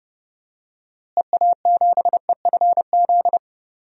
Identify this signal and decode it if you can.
Morse code, heard as a keyed tone, about 32 wpm, 700 Hz. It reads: EA7EFZ
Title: DXCC-100 Call Signs - 6 Characters - Repeated 2x - 30wpm